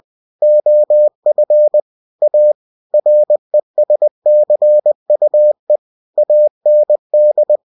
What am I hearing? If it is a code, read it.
Morse code, heard as a keyed tone, about 20 wpm, 600 Hz. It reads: OF A RESCUE AND
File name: cw-books / winnie-the-pooh / winnie-20wpm-page-0098